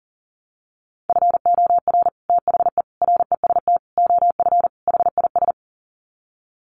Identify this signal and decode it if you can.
Morse code, heard as a keyed tone, about 40 wpm, 700 Hz. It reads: FOR THE REST OF HIS
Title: Sets of 5 Words 40wpm